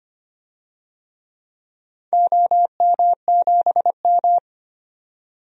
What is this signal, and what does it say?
Morse code, heard as a keyed tone, about 25 wpm, 700 Hz. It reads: OM7M
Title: DXCC-100 Call Signs - 4 Characters - 25wpm